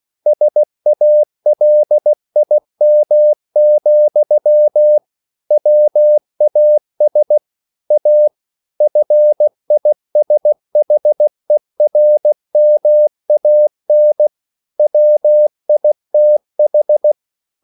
Morse code, heard as a keyed tone, about 16 wpm, 600 Hz. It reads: SALIM, WAS A FISHERMAN WITH